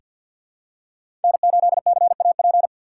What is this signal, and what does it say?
Morse code, heard as a keyed tone, about 48 wpm, 700 Hz. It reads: N8CAP
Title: USA Call Signs - 5 Characters - 3x Character Spacing - 50wpm